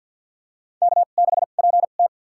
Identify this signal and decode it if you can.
Morse code, heard as a keyed tone, about 48 wpm, spaced as for 35 wpm, 700 Hz. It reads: K6PT
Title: USA Call Signs - 4 Characters - 2x Character Spacing - 50wpm